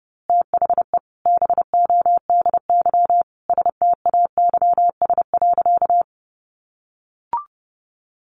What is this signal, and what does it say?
Morse code, heard as a keyed tone, about 30 wpm, 700 Hz. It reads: THE BODY STAYS.
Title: A Sound Of Thunder 30wpm